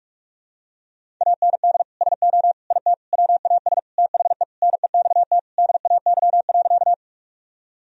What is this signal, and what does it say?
Morse code, heard as a keyed tone, about 45 wpm, 700 Hz. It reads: AND SO IT WAS THE NEXT DAY.
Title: Sentences from Top 1000 Words - Review Part 1 - 45wpm